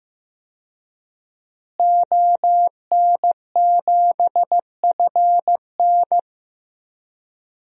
Morse code, heard as a keyed tone, about 15 wpm, 700 Hz. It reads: ON7FN
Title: DXCC-100 Call Signs - 5 Characters - Repeated 2x - 15wpm